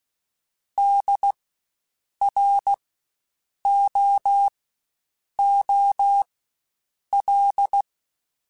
Morse code, heard as a keyed tone, about 16 wpm, spaced as for 7 wpm, 800 Hz. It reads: DROOL
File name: Spaced Alphas / Spaced Alphas mp3s / Ds sp @15